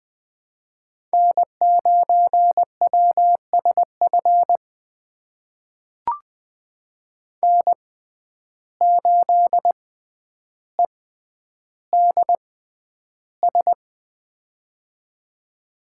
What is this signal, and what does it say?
Morse code, heard as a keyed tone, about 20 wpm, spaced as for 10 wpm, 700 Hz. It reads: N9WSF N8EDS
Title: USA Call Signs - 5 Characters - 6x Character Spacing - 20wpm